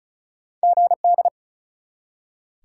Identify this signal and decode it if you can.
Morse code, heard as a keyed tone, about 36 wpm, 700 Hz. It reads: GD
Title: Top 180 QSO Elements 35wpm